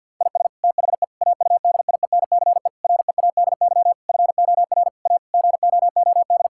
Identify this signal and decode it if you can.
Morse code, heard as a keyed tone, about 50 wpm, 700 Hz. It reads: IS THE AUDIENCE READY FOR A GOOD
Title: Sentences from Top 1000 Words - Review Part 5 - 50wpm